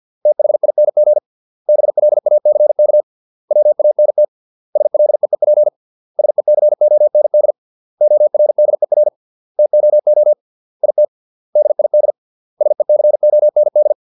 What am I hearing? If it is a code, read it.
Morse code, heard as a keyed tone, about 50 wpm, 600 Hz. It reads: THING BLACK WANT SLEEP SECOND ORDER TOO IT DID SECOND
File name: 632 words / 632 Words 50WPM